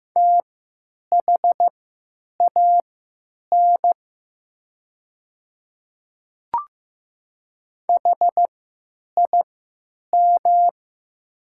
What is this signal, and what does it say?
Morse code, heard as a keyed tone, about 15 wpm, spaced as for 9 wpm, 700 Hz. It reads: THAN HIM
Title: 3x Character Spacing - Top 100 Words - 15wpm